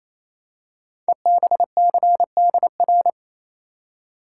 Morse code, heard as a keyed tone, about 27 wpm, 700 Hz. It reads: EBCDR
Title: Code Groups - 5 Letters - 28wpm